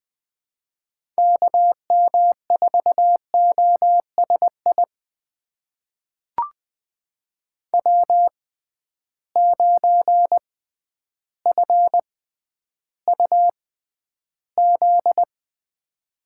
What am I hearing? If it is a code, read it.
Morse code, heard as a keyed tone, about 21 wpm, spaced as for 11 wpm, 700 Hz. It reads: KM4OSI W9FUZ
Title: USA Call Signs - 6x Character Spacing - 20wpm